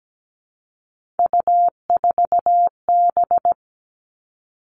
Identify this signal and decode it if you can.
Morse code, heard as a keyed tone, about 17 wpm, 700 Hz. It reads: U4B
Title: Three Challenging Letter-Number Combinations 17wpm